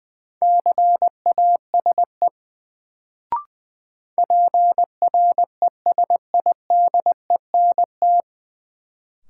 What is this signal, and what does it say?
Morse code, heard as a keyed tone, about 20 wpm, 700 Hz. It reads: CASE PRESIDENT
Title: Sets of 2 Words - New Word Drill - 20wpm